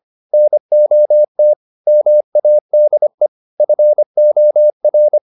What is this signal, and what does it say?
Morse code, heard as a keyed tone, about 25 wpm, 600 Hz. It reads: NOT MADE FOR